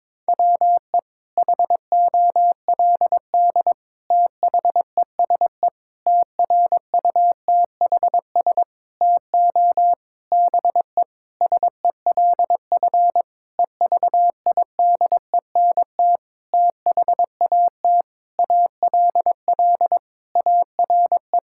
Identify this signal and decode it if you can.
Morse code, heard as a keyed tone, about 22 wpm, 700 Hz. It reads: WE HOLD THESE TRUTHS TO BE SELF EVIDENT THAT ALL ARE
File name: Sentences with 9-Word Phrases - 22wpm